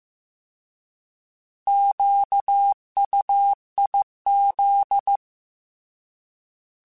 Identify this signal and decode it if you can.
Morse code, heard as a keyed tone, about 15 wpm, 800 Hz. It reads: QUIZ
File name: Alphabetical / Alphabetical mp3s / Qs @13